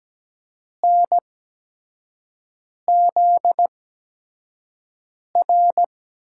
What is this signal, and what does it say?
Morse code, heard as a keyed tone, about 17 wpm, spaced as for 5 wpm, 700 Hz. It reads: NZR
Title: USA Call Signs - 8x Character Spacing - 17wpm